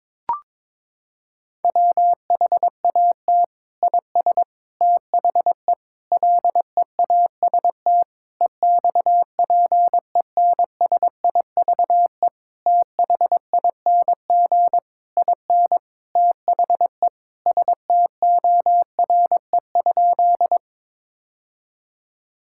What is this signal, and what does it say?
Morse code, heard as a keyed tone, about 22 wpm, 700 Hz. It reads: WHAT IS THE LEAST EXPENSIVE THING IN THE STORE?
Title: Sentences from Top 1000 Words - Review Part 6 - 22wpm